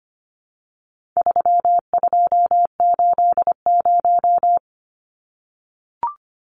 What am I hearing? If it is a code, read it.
Morse code, heard as a keyed tone, about 25 wpm, 700 Hz. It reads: 3280
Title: Numbers - 4-Digit 25wpm